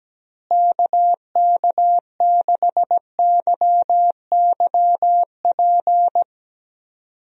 Morse code, heard as a keyed tone, about 17 wpm, 700 Hz. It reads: KK6YYP
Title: USA Call Signs - 6 Characters - Repeated 2x - 17wpm